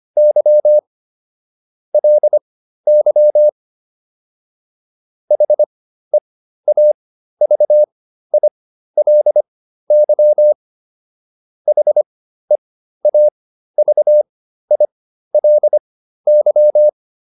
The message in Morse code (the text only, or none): Y LY HEAVILY HEAVILY